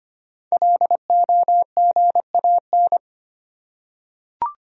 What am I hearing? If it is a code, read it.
Morse code, heard as a keyed tone, about 25 wpm, 700 Hz. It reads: LOGAN
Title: USA Common Names - Max 5 Letters - 25wpm